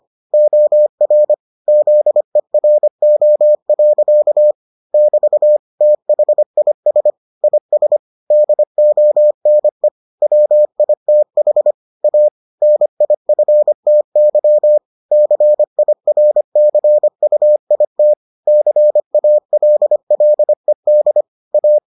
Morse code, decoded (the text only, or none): OR ZERO. <BT> THIS IS DONE WITH A NIFTY CIRCUIT CALLED A